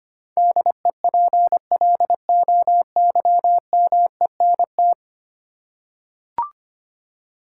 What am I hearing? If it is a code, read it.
Morse code, heard as a keyed tone, about 25 wpm, 700 Hz. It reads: DEPLOYMENT
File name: Ten Letter Words - 25wpm